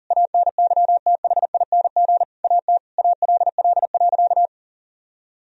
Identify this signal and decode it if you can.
Morse code, heard as a keyed tone, about 40 wpm, 700 Hz. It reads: ANYTHING AT ALL.
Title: Sentences from Top 1000 Words - Review Part 1 - 40wpm